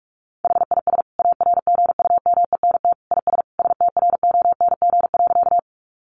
Morse code, heard as a keyed tone, about 44 wpm, 700 Hz. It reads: HIS ARGUMENT IS STRONG.
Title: Sentences from Top 1400 Words 45wpm